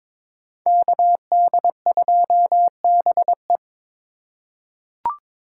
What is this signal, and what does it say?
Morse code, heard as a keyed tone, about 22 wpm, 700 Hz. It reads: KD2BE
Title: USA Call Signs - 5 Characters - 2x Character Spacing - 22wpm